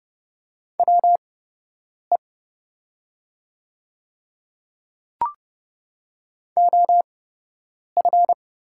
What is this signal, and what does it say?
Morse code, heard as a keyed tone, about 30 wpm, spaced as for 8 wpm, 700 Hz. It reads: WE OF
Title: 8x Character Spacing - Top 100 Words - 30wpm